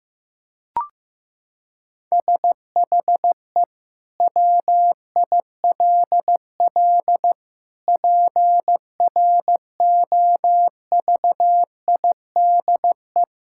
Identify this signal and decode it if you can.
Morse code, heard as a keyed tone, about 15 wpm, 700 Hz. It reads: SHE WILL PROVIDE
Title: Sentences from Top 1000 Words - Review Part 6 - 15wpm